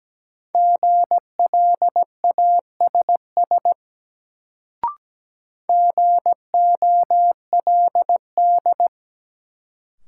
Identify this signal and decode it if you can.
Morse code, heard as a keyed tone, about 17 wpm, 700 Hz. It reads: GLASS GOLD
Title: Sentences from Top 1000 Words - Review Part 3 - New Word Drill - 17wpm